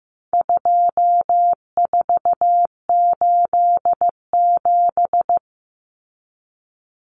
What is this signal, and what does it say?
Morse code, heard as a keyed tone, about 15 wpm, 700 Hz. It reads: 2487
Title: Numbers - 4-Digit 15wpm